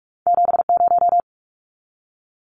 Morse code, heard as a keyed tone, about 46 wpm, 700 Hz. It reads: 70
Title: Numbers - 2-Digit 45wpm